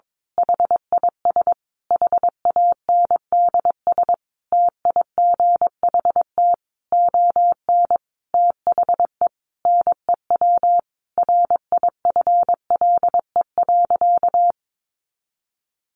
Morse code, heard as a keyed tone, about 22 wpm, 700 Hz. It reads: HIS HANDS TIGHT ON THE NEW RIFLE.
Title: A Sound Of Thunder 22wpm